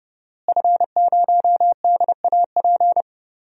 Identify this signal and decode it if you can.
Morse code, heard as a keyed tone, about 31 wpm, 700 Hz. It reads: F0DAP
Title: DXCC-100 Call Signs - 5 Characters - Repeated 2x - 30wpm